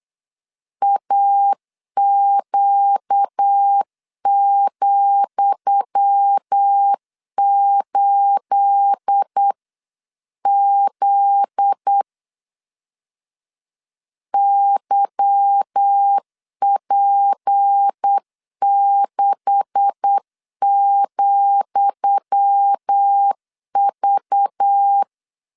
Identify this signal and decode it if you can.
Morse code, heard as a keyed tone, about 8 wpm, 800 Hz. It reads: AQ,8Z YP6,V
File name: LEZIONI / 10 SIGNS / 10D COMA 8wpm 3749